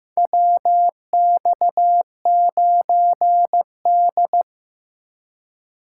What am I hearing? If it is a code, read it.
Morse code, heard as a keyed tone, about 16 wpm, 700 Hz. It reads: WX9D
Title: USA Call Signs - 4 Characters - 8x Character Spacing - 15wpm